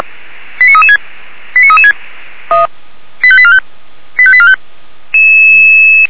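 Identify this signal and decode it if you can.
Five-tone selective call (ZVEI), heard as quick runs of five tones, tones of about 80 ms. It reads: 89336 89336 85845 85845
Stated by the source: Five-tone ZVEI selective call received by SDR.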